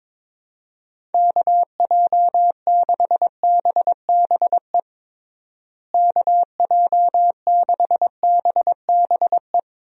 Morse code, heard as a keyed tone, about 21 wpm, 700 Hz. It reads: KJ6BBE KJ6BBE
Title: USA Call Signs - 6 Characters - Repeated 3x - 22wpm